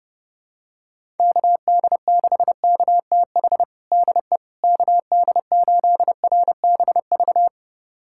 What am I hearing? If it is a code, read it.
Morse code, heard as a keyed tone, about 30 wpm, 700 Hz. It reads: KD6KTH DE KD8RBV
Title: USA Callsign DE Callsign - 6 Characters - 30wpm